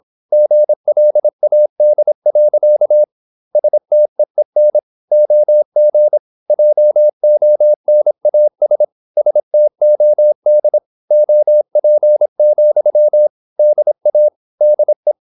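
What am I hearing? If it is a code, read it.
Morse code, heard as a keyed tone, about 26 wpm, 600 Hz. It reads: GLAD. STEEN OG JONAS STOD OP, DA DE
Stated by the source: High-speed (QRQ) Morse at 26 wpm.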